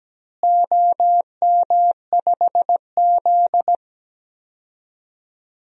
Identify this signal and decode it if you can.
Morse code, heard as a keyed tone, about 17 wpm, 700 Hz. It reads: OM5Z
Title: DXCC-100 Call Signs - 4 Characters - Repeated 3x - 17wpm